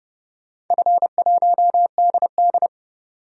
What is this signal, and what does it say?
Morse code, heard as a keyed tone, about 30 wpm, 700 Hz. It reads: F1DD
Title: DXCC-100 Call Signs - 4 Characters - 30wpm